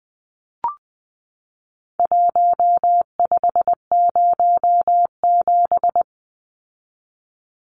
Morse code, heard as a keyed tone, about 20 wpm, 700 Hz. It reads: 1507
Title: Numbers - Contest Style - 4-Digit 20wpm